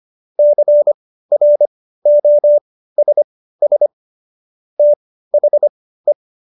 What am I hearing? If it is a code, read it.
Morse code, heard as a keyed tone, about 25 wpm, spaced as for 15 wpm, 590 Hz. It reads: CROSS THE
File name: Jokes-25-char-15-eff